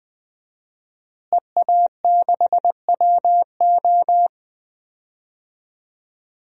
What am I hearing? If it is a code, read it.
Morse code, heard as a keyed tone, about 21 wpm, 700 Hz. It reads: EA6WO
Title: DXCC-100 Call Signs - 5 Characters - Repeated 3x - 20wpm